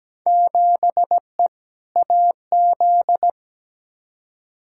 Morse code, heard as a keyed tone, about 17 wpm, 700 Hz. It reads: 7E AZ
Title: Field Day Exchanges Encore - 17wpm